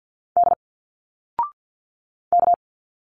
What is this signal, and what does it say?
Morse code, heard as a keyed tone, about 52 wpm, 700 Hz. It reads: D K
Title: Single Letters 50wpm